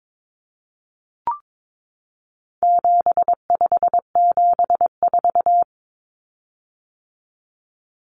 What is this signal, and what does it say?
Morse code, heard as a keyed tone, about 22 wpm, 700 Hz. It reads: 7574
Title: Common Cut Numbers - 4-Digit 22wpm